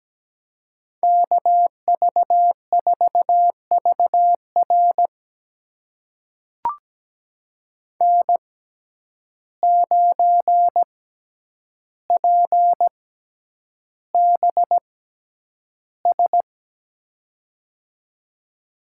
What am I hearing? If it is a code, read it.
Morse code, heard as a keyed tone, about 17 wpm, spaced as for 9 wpm, 700 Hz. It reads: KV4VR N9PBS
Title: USA Call Signs - 5 Characters - 6x Character Spacing - 17wpm